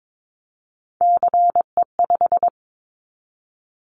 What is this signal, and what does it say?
Morse code, heard as a keyed tone, about 22 wpm, 700 Hz. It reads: CE5
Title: Callsign Prefixes for Top 100 DXCC - 22wpm